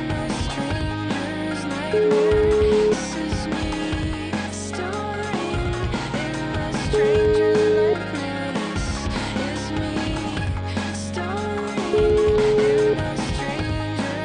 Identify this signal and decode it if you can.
Ringback tone (Europe, 425 Hz), with noise.